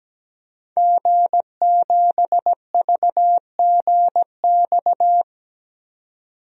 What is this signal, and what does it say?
Morse code, heard as a keyed tone, about 17 wpm, 700 Hz. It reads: G7VGX
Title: DXCC-100 Call Signs - 5 Characters - 17wpm